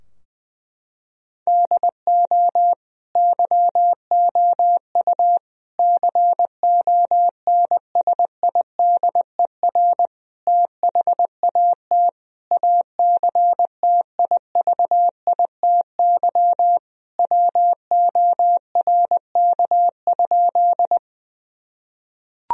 Morse code, heard as a keyed tone, about 20 wpm, 700 Hz. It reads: DO YOU CONSIDER THAT ACTIVITY WORK?
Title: Sentences from Top 1000 Words - Review Part 4 - 20wpm